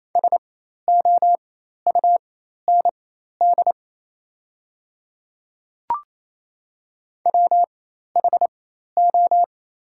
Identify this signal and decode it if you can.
Morse code, heard as a keyed tone, about 28 wpm, spaced as for 13 wpm, 700 Hz. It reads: SOUND WHO
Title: 4x Character Spacing - Top 100 Words - 28wpm